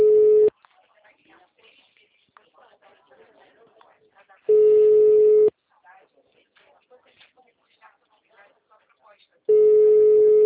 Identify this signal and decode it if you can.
Ringback tone (Europe, 425 Hz), with speech.
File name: out- -2002- - - .21184